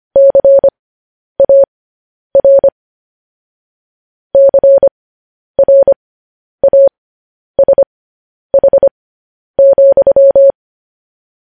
Morse code, heard as a keyed tone, about 25 wpm, spaced as for 10 wpm, 550 Hz. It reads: CAR CRASH,
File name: podcast / News headlines-1x-25x10